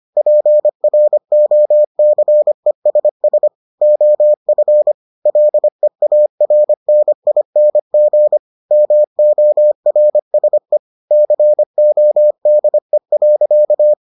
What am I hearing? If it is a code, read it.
Morse code, heard as a keyed tone, about 25 wpm, 600 Hz. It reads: PROCESS OF LEARNING MORSE CODE.